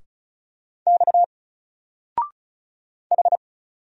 Morse code, heard as a keyed tone, about 35 wpm, 700 Hz. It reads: X H
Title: Single Letter-Number 35wpm